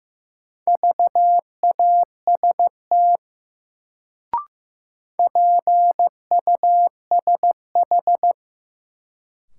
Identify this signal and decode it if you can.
Morse code, heard as a keyed tone, about 15 wpm, 700 Hz. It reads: VAST PUSH
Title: Sentences from Top 1300 Words - New Word Drill - 15wpm